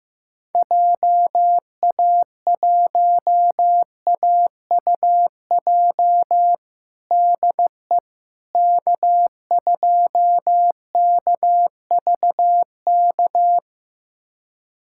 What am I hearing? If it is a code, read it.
Morse code, heard as a keyed tone, about 15 wpm, 700 Hz. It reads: JA1AUJ DE K2KVK
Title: USA Callsign DE Callsign 15wpm